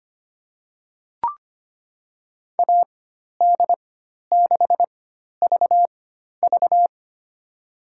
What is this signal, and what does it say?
Morse code, heard as a keyed tone, about 25 wpm, spaced as for 12 wpm, 700 Hz. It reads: AD6VV
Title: USA Call Signs - 5 Characters - 4x Character Spacing - 25wpm